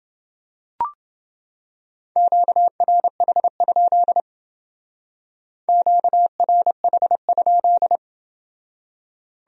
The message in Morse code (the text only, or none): QRH? QRH?